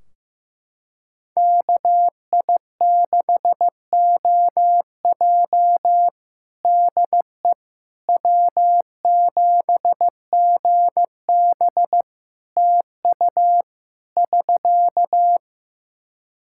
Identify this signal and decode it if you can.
Morse code, heard as a keyed tone, about 15 wpm, 700 Hz. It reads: KI6OJ DE W7GB TU <SK>
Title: Practice QSO - Exchange 4 - Ending - 15wpm